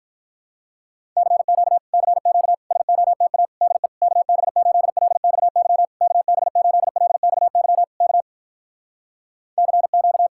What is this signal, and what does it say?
Morse code, heard as a keyed tone, about 46 wpm, 700 Hz. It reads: CQ CQ SOTA DE KB8LXY KB8LXY K CQ